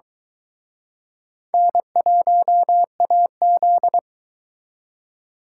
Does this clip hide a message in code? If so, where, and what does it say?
Yes: Morse code from 1.5 to 4.0 s, heard as a keyed tone, about 22 wpm, 700 Hz. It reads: N1AZ